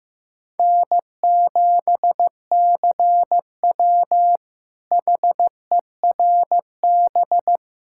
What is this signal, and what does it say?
Morse code, heard as a keyed tone, about 15 wpm, 700 Hz. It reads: N7CW HERB
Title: K1USN SST QSO Practice - 15wpm